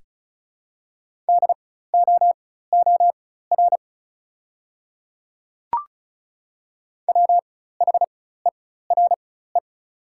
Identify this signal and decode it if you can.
Morse code, heard as a keyed tone, about 36 wpm, spaced as for 16 wpm, 700 Hz. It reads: DOOR WHERE
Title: 4x Character Spacing - Top 300 Words - 35wpm